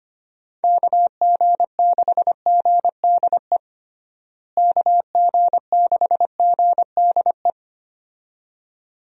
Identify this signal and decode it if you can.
Morse code, heard as a keyed tone, about 25 wpm, 700 Hz. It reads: KG6GDE KG6GDE
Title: USA Call Signs - 6 Characters - Repeated 2x - 25wpm